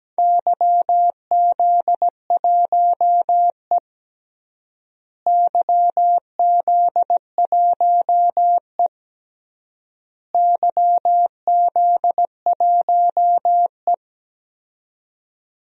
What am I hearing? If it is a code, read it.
Morse code, heard as a keyed tone, about 17 wpm, 700 Hz. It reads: YZ1E YZ1E YZ1E
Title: DXCC-100 Call Signs - 4 Characters - Repeated 3x - 17wpm